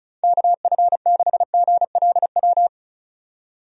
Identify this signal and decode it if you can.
Morse code, heard as a keyed tone, about 36 wpm, 700 Hz. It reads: KF6GLW
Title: USA Call Signs - 6 Characters - 35wpm